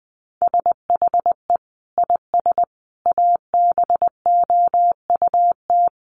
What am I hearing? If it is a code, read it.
Morse code, heard as a keyed tone, about 20 wpm, 700 Hz. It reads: SHE IS ABOUT
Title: Sentences from Top 1300 Words 20wpm